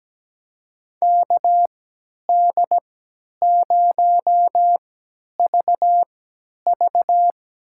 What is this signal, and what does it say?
Morse code, heard as a keyed tone, about 17 wpm, spaced as for 10 wpm, 700 Hz. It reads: KD0VV
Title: USA Call Signs - 6 Characters - 3x Character Spacing - 17wpm